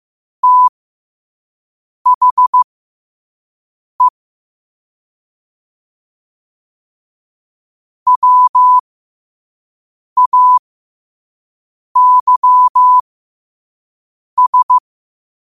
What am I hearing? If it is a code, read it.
Morse code, heard as a keyed tone, about 15 wpm, spaced as for 5 wpm, 1000 Hz. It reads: THE WAYS
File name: qotdfiles / 5wpm